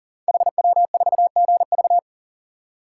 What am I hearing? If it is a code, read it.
Morse code, heard as a keyed tone, about 40 wpm, 700 Hz. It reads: HW4GV
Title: Code Groups - 5 Letter:Numbers - 40wpm